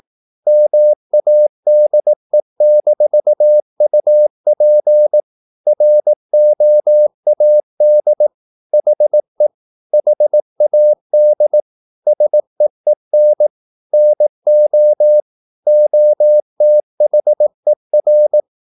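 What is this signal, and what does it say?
Morse code, heard as a keyed tone, about 18 wpm, 600 Hz. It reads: MADE-UP ROAD HE HAD SEEN NO OTHER
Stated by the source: Morse practice at 18 wpm, English text, from CW med Gnister.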